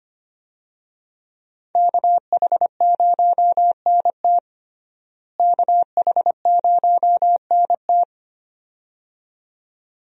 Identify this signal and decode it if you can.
Morse code, heard as a keyed tone, about 25 wpm, 700 Hz. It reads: KH0NT KH0NT
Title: USA Call Signs - 5 Characters - Repeated 2x - 25wpm